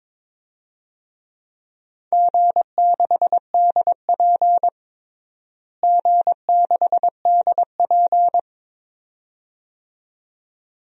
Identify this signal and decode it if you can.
Morse code, heard as a keyed tone, about 22 wpm, 700 Hz. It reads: G6DP G6DP
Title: DXCC-100 Call Signs - 4 Characters - Repeated 2x - 22wpm